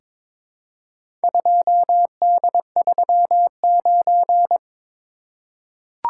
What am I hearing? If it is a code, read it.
Morse code, heard as a keyed tone, about 22 wpm, 700 Hz. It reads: 2D39
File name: practice / 22wpm / Four Letter-Number Combinations - 22wpm